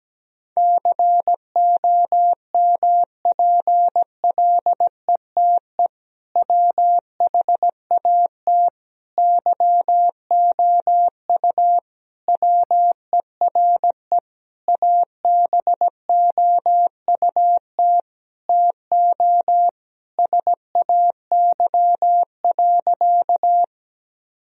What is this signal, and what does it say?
Morse code, heard as a keyed tone, about 17 wpm, 700 Hz. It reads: COMPLETE WHAT YOU WERE ABOUT TO SAY.
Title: Sentences from Top 1000 Words - Review Part 2 - 17wpm